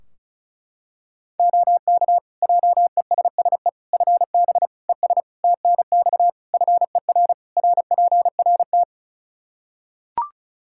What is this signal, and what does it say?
Morse code, heard as a keyed tone, about 36 wpm, 700 Hz. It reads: OK JESSE FB ES TNX FER RPRT
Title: Practice QSO - Exchange 2 - Info - 35wpm